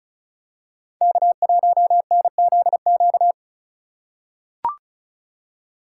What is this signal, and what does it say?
Morse code, heard as a keyed tone, about 36 wpm, 700 Hz. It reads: K1NZQ
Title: USA Call Signs - 5 Characters - 4x Character Spacing - 35wpm